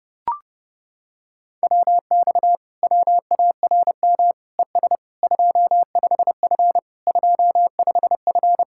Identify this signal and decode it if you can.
Morse code, heard as a keyed tone, about 30 wpm, 700 Hz. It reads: WX WARM ES 25F 25F